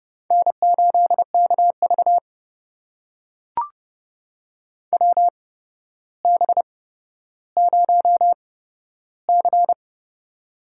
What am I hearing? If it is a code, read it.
Morse code, heard as a keyed tone, about 30 wpm, spaced as for 13 wpm, 700 Hz. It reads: N8KV WB0C
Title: USA Call Signs - 4 Characters - 8x Character Spacing - 30wpm